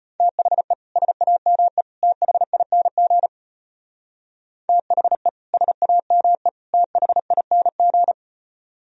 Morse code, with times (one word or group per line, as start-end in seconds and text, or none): 0.2-0.7 THE
1.0-1.8 SAME
2.0-3.3 THING
4.7-5.3 THE
5.5-6.5 SAME
6.7-8.1 THING